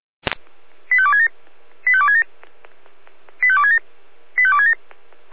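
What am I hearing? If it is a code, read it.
Five-tone selective call (ZVEI), heard as quick runs of five tones, tones of about 70 ms. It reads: 85267 85267 85267 85267
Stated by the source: Five-tone ZVEI selective call received by SDR.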